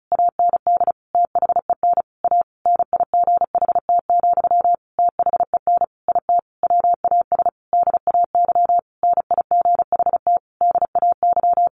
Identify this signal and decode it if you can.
Morse code, heard as a keyed tone, about 34 wpm, 700 Hz. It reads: AND THEN A NIGHT, THEN IT WAS DAY NIGHT DAY
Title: A Sound Of Thunder 35wpm